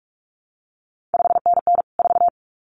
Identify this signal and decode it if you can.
Morse code, heard as a keyed tone, about 46 wpm, 700 Hz. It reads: 5NN 4
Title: CQ WW DX Contest Exchange - 45wpm